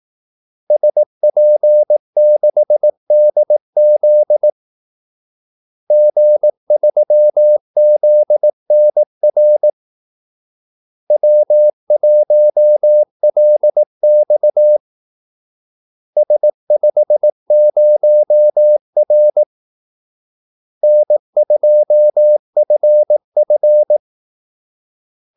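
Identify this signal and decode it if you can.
Morse code, heard as a keyed tone, about 18 wpm, 600 Hz. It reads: SP6DZ G3ZNR W1LX S50R N2FF
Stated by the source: Morse call-sign practice at 18 wpm.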